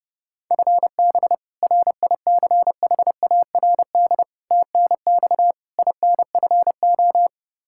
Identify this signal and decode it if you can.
Morse code, heard as a keyed tone, about 32 wpm, 700 Hz. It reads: FB RICHARD TNX INFO
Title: Practice QSO - Exchange 3 - More Info - 30wpm